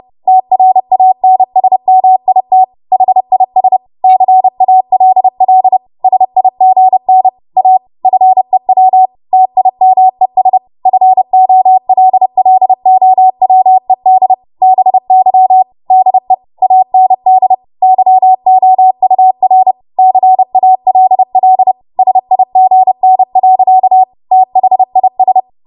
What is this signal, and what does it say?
Morse code, heard as a keyed tone, about 30 wpm, 750 Hz. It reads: TRANSMIT HIS CALL SIGN A FEW TIMES FOLLOWED BY DE AND YOUR CALL SIGN. THIS
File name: Morse / 30WPM